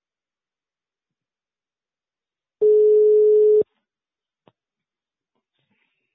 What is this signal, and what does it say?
Ringback tone (Europe, 425 Hz).